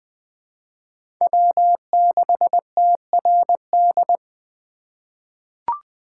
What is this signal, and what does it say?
Morse code, heard as a keyed tone, about 20 wpm, 700 Hz. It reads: W6TRD
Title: USA Call Signs - 5 Characters - 4x Character Spacing - 20wpm